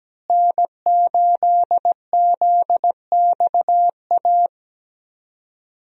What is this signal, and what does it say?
Morse code, heard as a keyed tone, about 17 wpm, 700 Hz. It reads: N8ZXA
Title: USA Call Signs - 5 Characters - Repeated 3x - 17wpm